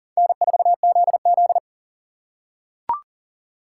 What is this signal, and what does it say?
Morse code, heard as a keyed tone, about 39 wpm, 700 Hz. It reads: N4ZZ
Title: K1USN SST QSO Practice - 40wpm